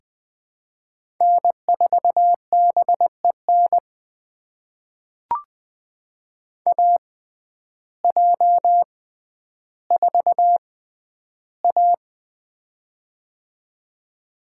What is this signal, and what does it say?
Morse code, heard as a keyed tone, about 21 wpm, spaced as for 11 wpm, 700 Hz. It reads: N4BEN AJ4A
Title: USA Call Signs - 6x Character Spacing - 20wpm